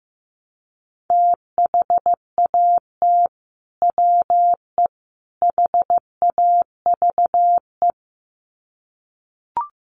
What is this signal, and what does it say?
Morse code, heard as a keyed tone, about 15 wpm, 700 Hz. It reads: THAT WE HAVE
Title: Sets of 3 Words 15wpm